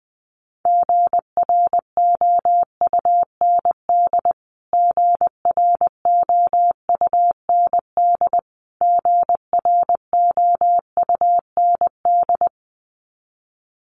Morse code, heard as a keyed tone, about 20 wpm, 700 Hz. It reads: GROUND GROUND GROUND
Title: Top 500 Words Repeated 20wpm